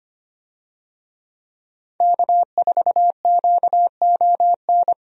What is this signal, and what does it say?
Morse code, heard as a keyed tone, about 25 wpm, 700 Hz. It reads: K4QON
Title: USA Call Signs - 5 Characters - Repeated 2x - 25wpm